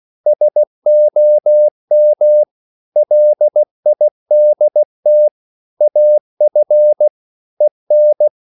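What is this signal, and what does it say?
Morse code, heard as a keyed tone, about 16 wpm, 600 Hz. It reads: SOM LIDT AF EN
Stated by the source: Slow (QRS) Morse code at 16 wpm, from CW med Gnister.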